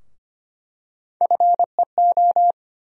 Morse code, heard as a keyed tone, about 25 wpm, 700 Hz. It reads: FEO
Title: USA Call Sign Postfixes - 3 Letters 25wpm